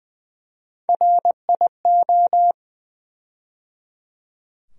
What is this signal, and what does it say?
Morse code, heard as a keyed tone, about 20 wpm, 700 Hz. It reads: RIO
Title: USA Call Sign Postfixes - 3 Letters 20wpm